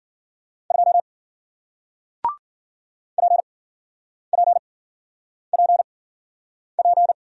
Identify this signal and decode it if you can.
Morse code, heard as a keyed tone, about 54 wpm, spaced as for 9 wpm, 700 Hz. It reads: 2 PPPP